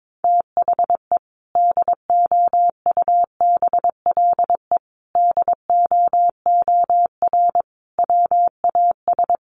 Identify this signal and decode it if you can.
Morse code, heard as a keyed tone, about 22 wpm, 700 Hz. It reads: THE DOUBLE DOOR WAS
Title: Sentences from Top 1400 Words 22wpm